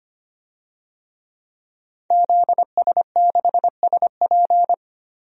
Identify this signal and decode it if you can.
Morse code, heard as a keyed tone, about 25 wpm, 700 Hz. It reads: ZS6SP
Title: DXCC-100 Call Signs - 5 Characters - Repeated 3x - 25wpm